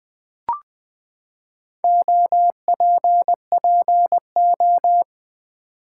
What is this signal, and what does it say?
Morse code, heard as a keyed tone, about 20 wpm, 700 Hz. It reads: OPPO
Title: Build A Word - 4 Letters At A Time - 8-Letter Words - 20wpm